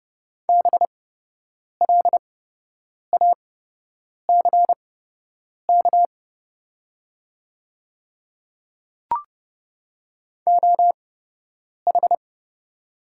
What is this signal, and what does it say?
Morse code, heard as a keyed tone, about 29 wpm, spaced as for 8 wpm, 700 Hz. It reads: BLACK OH
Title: 8x Character Spacing - Top 500 Words - 30wpm